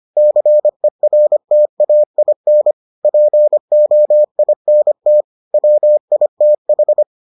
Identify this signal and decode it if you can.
Morse code, heard as a keyed tone, about 25 wpm, 600 Hz. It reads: CERTAIN POINT WITH